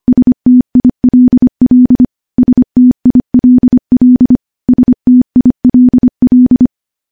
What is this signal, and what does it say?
Morse code, heard as a keyed tone, about 25 wpm, 250 Hz. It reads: STILL STILL STILL
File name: morse / morse-training-25WPM